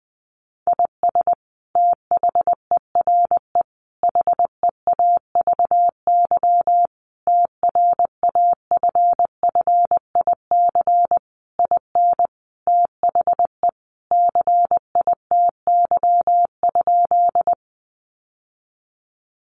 Morse code, heard as a keyed tone, about 20 wpm, 700 Hz. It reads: IS THERE HEAVY TRAFFIC IN THE CITY?
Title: Sentences from Top 1300 Words 20wpm